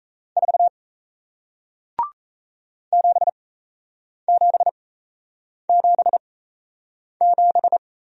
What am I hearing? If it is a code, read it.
Morse code, heard as a keyed tone, about 38 wpm, spaced as for 8 wpm, 700 Hz. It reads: 4 7777